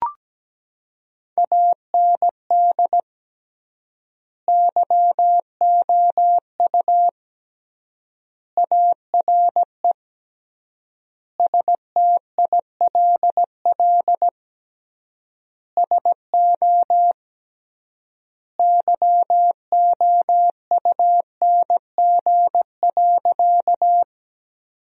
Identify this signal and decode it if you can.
Morse code, heard as a keyed tone, about 17 wpm, 700 Hz. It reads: AND YOU ARE STILL SO YOUNG.
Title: 3x Word-Spacing - Sentences from Top 400 Words - 17wpm